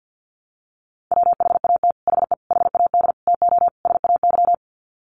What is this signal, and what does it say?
Morse code, heard as a keyed tone, about 50 wpm, 700 Hz. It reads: WHAT HE HAD TO SAY
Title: Sets of 5 Words Encore2 50wpm